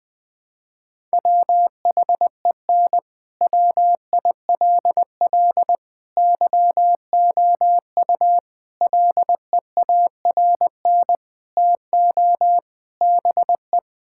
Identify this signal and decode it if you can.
Morse code, heard as a keyed tone, about 21 wpm, 700 Hz. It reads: WHEN WILL YOU LEARN TO BE
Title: Sentences from Top 1000 Words - Review Part 6 - 20wpm